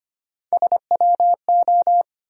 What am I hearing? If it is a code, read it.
Morse code, heard as a keyed tone, about 25 wpm, 700 Hz. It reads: SWO
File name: Three Letter-Number Combinations - 25wpm